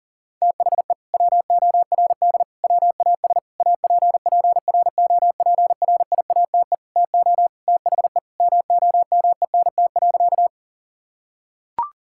Morse code, heard as a keyed tone, about 37 wpm, 700 Hz. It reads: THE WORD WAS APPROPRIATE TO THE MOMENT.
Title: Sentences from Top 1000 Words - Review Part 5 - 40wpm